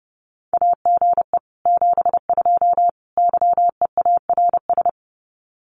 Morse code, heard as a keyed tone, about 30 wpm, 700 Hz. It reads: AGE 72 YEARS
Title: Common QSO Phrases Encore 30wpm